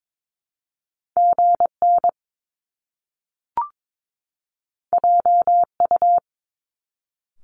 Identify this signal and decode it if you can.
Morse code, heard as a keyed tone, about 22 wpm, 700 Hz. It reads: GN JU
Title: Top Two Letter Combinations in Words 22wpm